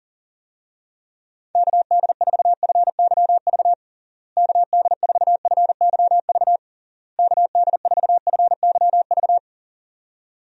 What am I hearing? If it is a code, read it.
Morse code, heard as a keyed tone, about 40 wpm, 700 Hz. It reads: KD4FYV KD4FYV KD4FYV
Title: USA Call Signs - 6 Characters - Repeated 3x - 40wpm